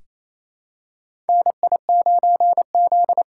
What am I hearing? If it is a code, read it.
Morse code, heard as a keyed tone, about 27 wpm, 700 Hz. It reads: NI9Z
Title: USA Callsign DE Callsign - 4 Characters - 28wpm